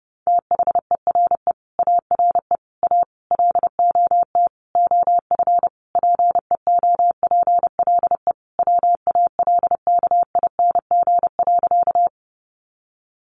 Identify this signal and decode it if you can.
Morse code, heard as a keyed tone, about 30 wpm, 700 Hz. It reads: THERE ARE A LOT OF PEOPLE WALKING.
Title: Easy Phrases 30wpm